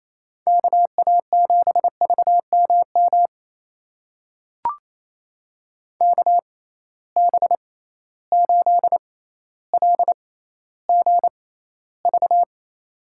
Morse code, heard as a keyed tone, about 28 wpm, spaced as for 14 wpm, 700 Hz. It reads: KA7VMM KB8LGV
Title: USA Call Signs - 6 Characters - 6x Character Spacing - 28wpm